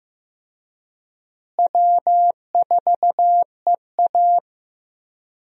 Morse code, heard as a keyed tone, about 15 wpm, 700 Hz. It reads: W4EA
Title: USA Call Signs - 4 Characters - Repeated 3x - 15wpm